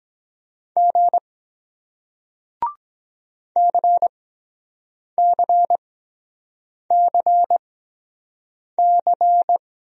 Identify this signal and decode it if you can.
Morse code, heard as a keyed tone, about 23 wpm, spaced as for 7 wpm, 700 Hz. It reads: G CCCC